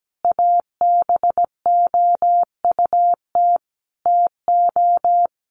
Morse code, heard as a keyed tone, about 17 wpm, 700 Hz. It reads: ABOUT TO
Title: Sentences from Top 1300 Words 17wpm